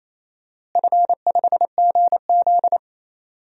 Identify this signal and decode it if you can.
Morse code, heard as a keyed tone, about 27 wpm, 700 Hz. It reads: F5GZ
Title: DXCC-100 Call Signs - 4 Characters - Repeated 3x - 28wpm